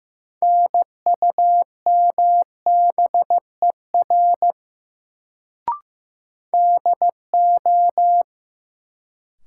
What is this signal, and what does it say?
Morse code, heard as a keyed tone, about 15 wpm, 700 Hz. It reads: NUMBER DO
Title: Sets of 2 Words - New Word Drill - 15wpm